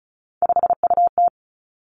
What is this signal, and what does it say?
Morse code, heard as a keyed tone, about 36 wpm, 700 Hz. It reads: 5UT
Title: Three Challenging Letter-Number Combinations 35wpm